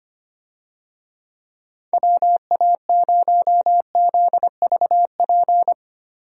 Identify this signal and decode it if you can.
Morse code, heard as a keyed tone, about 25 wpm, 700 Hz. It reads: WA0ZVP
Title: USA Call Signs - 6 Characters - Repeated 2x - 25wpm